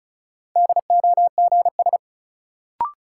Morse code, heard as a keyed tone, about 36 wpm, 700 Hz. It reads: DOGS
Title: Sentences from Top 1600 Words - New Word Drill Alt - 35wpm